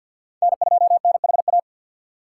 Morse code, heard as a keyed tone, about 52 wpm, 700 Hz. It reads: N2NHA
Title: USA Call Signs - 5 Characters - 4x Character Spacing - 50wpm